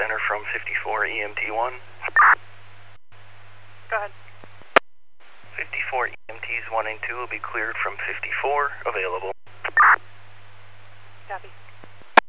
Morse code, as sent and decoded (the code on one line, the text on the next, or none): none